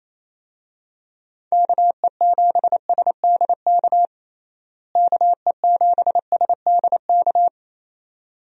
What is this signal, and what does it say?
Morse code, heard as a keyed tone, about 28 wpm, 700 Hz. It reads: KE7SDK KE7SDK
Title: USA Call Signs - 6 Characters - Repeated 2x - 28wpm